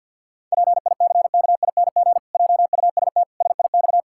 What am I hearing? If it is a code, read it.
Morse code, heard as a keyed tone, about 47 wpm, 700 Hz. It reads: PICKING JUST SIX
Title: Pangram Challenge v2 - 50wpm